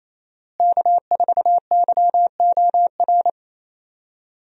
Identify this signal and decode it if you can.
Morse code, heard as a keyed tone, about 28 wpm, 700 Hz. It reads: K4YOR
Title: USA Call Signs - 5 Characters - Repeated 2x - 28wpm